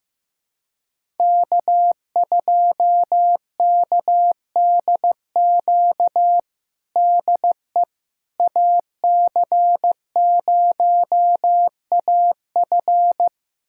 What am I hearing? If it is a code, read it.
Morse code, heard as a keyed tone, about 16 wpm, 700 Hz. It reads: K2KDQ DE AC0AF
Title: USA Callsign DE Callsign - 5 Characters - 15wpm